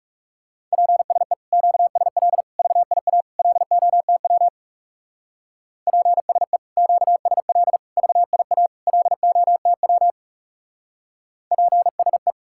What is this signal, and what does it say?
Morse code, heard as a keyed tone, about 37 wpm, 700 Hz. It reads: PSE QSL VIA LOTW PSE QSL VIA LOTW PSE
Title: Speed-Racing Common QSO Phrases - 30wpm